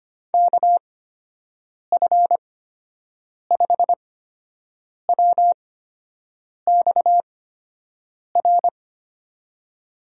Morse code, heard as a keyed tone, about 25 wpm, spaced as for 7 wpm, 700 Hz. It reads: KF5WXR
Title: USA Call Signs - 6 Characters - 8x Character Spacing - 25wpm